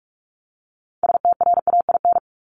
Morse code, heard as a keyed tone, about 44 wpm, 700 Hz. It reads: STRAIN
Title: Top 1751-2000 Words 45wpm